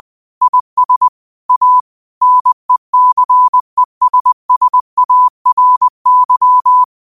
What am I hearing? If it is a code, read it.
Morse code, heard as a keyed tone, about 20 wpm, 1000 Hz. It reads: IS A NECESSARY
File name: qotdfiles / 20wpm